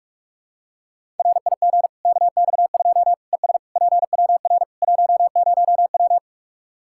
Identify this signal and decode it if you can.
Morse code, heard as a keyed tone, about 46 wpm, 700 Hz. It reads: RIG KX2 ES PWR 10W
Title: Practice QSO - Exchange 2 - Info - 45wpm